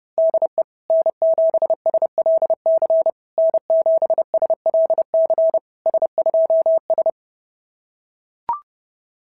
Morse code, heard as a keyed tone, about 26 wpm, 650 Hz. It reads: DE N7SLC N7SLC S2S
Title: SOTA QSO Practice 30wpm